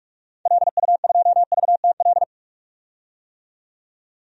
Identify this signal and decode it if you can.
Morse code, heard as a keyed tone, about 46 wpm, 690 Hz. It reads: LU2VTL
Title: DXCC-100 Call Signs - 6 Characters - 45wpm